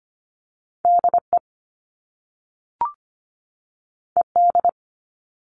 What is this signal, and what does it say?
Morse code, heard as a keyed tone, about 25 wpm, 700 Hz. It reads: D E E D
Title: All Cut Numbers - 2-Digit 25wpm